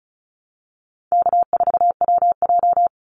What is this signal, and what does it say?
Morse code, heard as a keyed tone, about 34 wpm, 700 Hz. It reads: K4WJ
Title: Easy Call Signs Repeated 3x - 35wpm